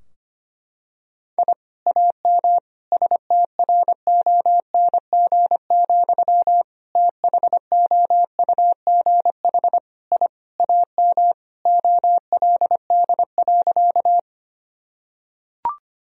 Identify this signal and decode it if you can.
Morse code, heard as a keyed tone, about 25 wpm, 700 Hz. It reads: I AM STRONG, THOUGH I AM OLD.
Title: Sentences from Top 1000 Words - Review Part 3 - 25wpm